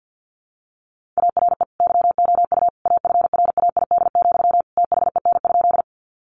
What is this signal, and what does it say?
Morse code, heard as a keyed tone, about 50 wpm, 700 Hz. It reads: ARE YOU AFRAID, THEN?
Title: Sentences from Top 1500 Words 50wpm